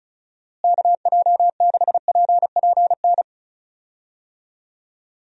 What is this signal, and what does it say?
Morse code, heard as a keyed tone, about 34 wpm, 700 Hz. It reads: KJ6PPN
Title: USA Call Signs - 6 Characters - Repeated 2x - 35wpm